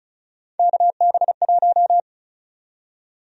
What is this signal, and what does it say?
Morse code, heard as a keyed tone, about 36 wpm, 700 Hz. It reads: KB1
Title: USA Call Sign Prefixes - Repeated 2x - 35wpm